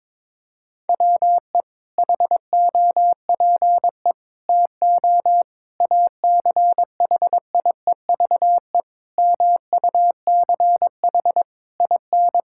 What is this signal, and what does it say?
Morse code, heard as a keyed tone, about 22 wpm, 700 Hz. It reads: WE HOPE TO ACHIEVE MUCH IN
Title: Sentences from Top 1000 Words - Review Part 6 - 22wpm